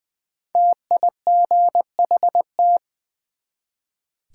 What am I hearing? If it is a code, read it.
Morse code, heard as a keyed tone, about 20 wpm, 700 Hz. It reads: TIGHT
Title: Build A Word - 1 Letter At A Time - 5-Letter Words - 20wpm